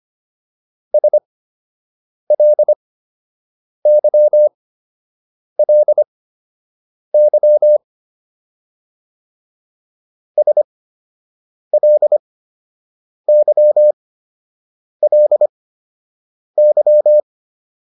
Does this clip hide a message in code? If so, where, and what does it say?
Yes: Morse code from 0.9 to 17.2 s, heard as a keyed tone, about 25 wpm, spaced as for 7 wpm, 600 Hz. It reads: SLYLY SLYLY